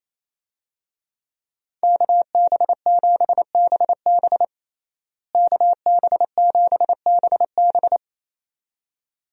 Morse code, heard as a keyed tone, about 28 wpm, 700 Hz. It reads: KB7BB KB7BB
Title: USA Call Signs - 5 Characters - Repeated 2x - 28wpm